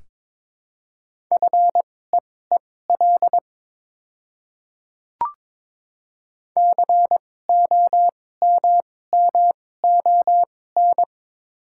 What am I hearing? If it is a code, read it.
Morse code, heard as a keyed tone, about 22 wpm, spaced as for 16 wpm, 700 Hz. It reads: FEEL COMMON
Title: 2x Character Spacing - Top 500 Words - 22wpm